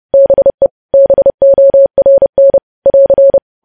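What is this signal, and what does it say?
Morse code, heard as a keyed tone, about 30 wpm, 550 Hz. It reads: BE BORN <AR>
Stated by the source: Morse news headlines sent at 30 wpm, opening with vvv.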